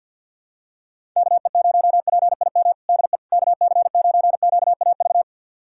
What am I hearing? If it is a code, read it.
Morse code, heard as a keyed tone, about 51 wpm, 690 Hz. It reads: KE0PIM DE KC9QAV
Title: USA Callsign DE Callsign - 6 Characters - 50wpm